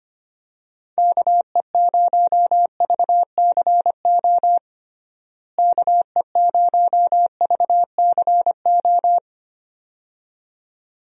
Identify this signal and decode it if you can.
Morse code, heard as a keyed tone, about 25 wpm, 700 Hz. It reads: KE0VCO KE0VCO
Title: USA Call Signs - 6 Characters - Repeated 2x - 25wpm